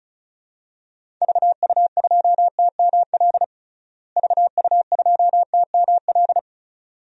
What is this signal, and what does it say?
Morse code, heard as a keyed tone, about 34 wpm, 700 Hz. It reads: VU2TML VU2TML
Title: DXCC-100 Call Signs - 6 Characters - Repeated 2x - 35wpm